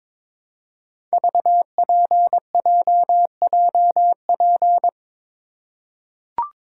Morse code, heard as a keyed tone, about 22 wpm, 700 Hz. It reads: VPJJP
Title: Code Groups - 5 Letters - 22wpm